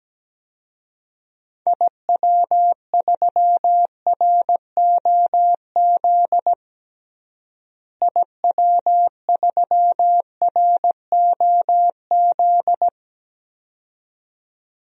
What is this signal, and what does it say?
Morse code, heard as a keyed tone, about 17 wpm, 700 Hz. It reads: IW3ROZ IW3ROZ
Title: DXCC-100 Call Signs - 6 Characters - Repeated 2x - 17wpm